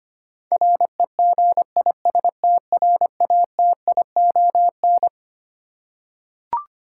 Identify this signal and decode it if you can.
Morse code, heard as a keyed tone, about 25 wpm, 700 Hz. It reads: REGISTRATION
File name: Twelve Letter Words - 25wpm